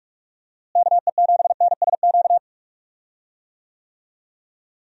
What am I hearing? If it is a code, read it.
Morse code, heard as a keyed tone, about 44 wpm, 700 Hz. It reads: KE7NSQ
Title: USA Call Signs - 6 Characters - Repeated 3x - 45wpm